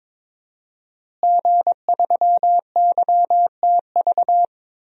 Morse code, heard as a keyed tone, about 22 wpm, 700 Hz. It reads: G3YTV
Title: DXCC-100 Call Signs - 5 Characters - Repeated 3x - 22wpm